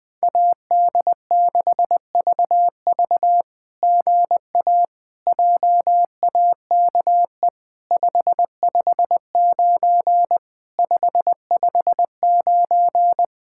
Morse code, heard as a keyed tone, about 20 wpm, 700 Hz. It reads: AD6VV GA JAKE 559 559
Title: SOTA QSO Practice 20wpm